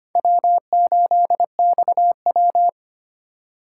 Morse code, heard as a keyed tone, about 33 wpm, 700 Hz. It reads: W8XW